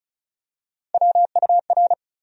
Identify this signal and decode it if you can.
Morse code, heard as a keyed tone, about 36 wpm, 700 Hz. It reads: WUR
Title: USA Call Sign Postfixes - 3 Letters 35wpm